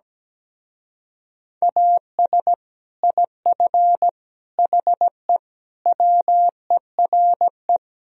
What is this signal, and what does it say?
Morse code, heard as a keyed tone, about 17 wpm, 700 Hz. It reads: AS IF HE WERE
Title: Speed-Racing - Sets of 4 Words 17wpm